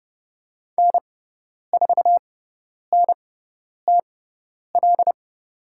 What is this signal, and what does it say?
Morse code, heard as a keyed tone, about 30 wpm, spaced as for 10 wpm, 700 Hz. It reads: N4NTL
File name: Lesson 17g - Letter u - Call Signs 30wpm